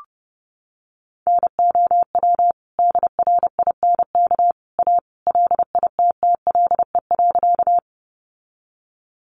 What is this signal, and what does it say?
Morse code, heard as a keyed tone, about 30 wpm, 700 Hz. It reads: NOW DRINK A LITTLE.
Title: Sentences from Top 1500 Words 30wpm